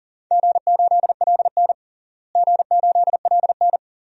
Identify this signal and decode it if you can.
Morse code, heard as a keyed tone, about 40 wpm, 700 Hz. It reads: G8LN G8LN
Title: DXCC-100 Call Signs - 4 Characters - Repeated 2x - 40wpm